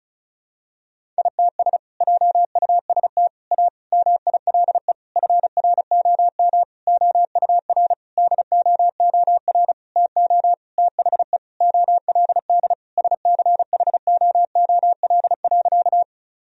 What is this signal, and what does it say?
Morse code, heard as a keyed tone, about 36 wpm, 700 Hz. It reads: ITS JUST A MILE FROM OUR DOOR TO THE OLD SCHOOL.
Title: Sentences from Top 1000 Words - Review Part 1 - 35wpm